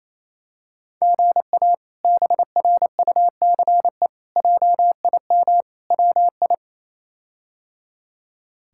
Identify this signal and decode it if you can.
Morse code, heard as a keyed tone, about 27 wpm, 700 Hz. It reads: GA BRUCE JIM WI
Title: K1USN SST QSO Practice - 28wpm